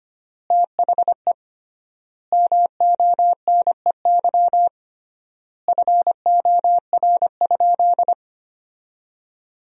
Morse code, heard as a keyed tone, about 25 wpm, 700 Hz. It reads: THE MONEY FOR?
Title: 3x Word-Spacing - Sentences from Top 400 Words - 25wpm